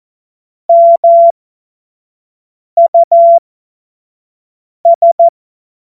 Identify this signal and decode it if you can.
Morse code, heard as a keyed tone, about 14 wpm, spaced as for 5 wpm, 680 Hz. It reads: MUS
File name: Morsekurs / Witze / Witz 06 Februar 2024 25 F70